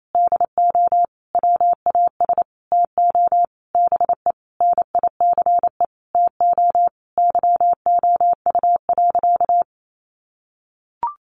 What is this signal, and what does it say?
Morse code, heard as a keyed tone, about 28 wpm, 700 Hz. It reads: DO WAS TO BE NICE TO YOU.
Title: Sentences from Top 1100 Words 28wpm